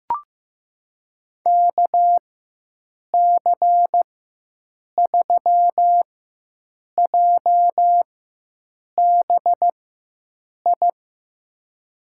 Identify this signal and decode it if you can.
Morse code, heard as a keyed tone, about 15 wpm, spaced as for 7 wpm, 700 Hz. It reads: KC3JBI
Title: USA Call Signs - 4x Character Spacing - 15wpm